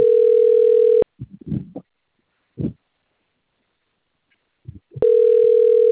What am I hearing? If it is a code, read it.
Ringback tone (North America), 440 + 480 Hz.
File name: out- - - - - .3167